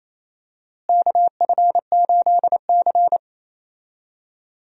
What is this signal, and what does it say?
Morse code, heard as a keyed tone, about 28 wpm, 700 Hz. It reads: KF8C
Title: USA Call Signs - 4 Characters - Repeated 3x - 28wpm